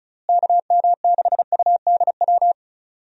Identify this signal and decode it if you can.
Morse code, heard as a keyed tone, about 36 wpm, 700 Hz. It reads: KM6UDW
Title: USA Call Signs - 6 Characters - Repeated 3x - 35wpm